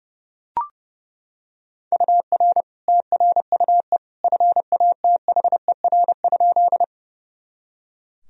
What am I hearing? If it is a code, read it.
Morse code, heard as a keyed tone, about 30 wpm, 700 Hz. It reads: UR TRUE FATHER?
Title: Sentences from Top 400 Words and Abbreviations 30wpm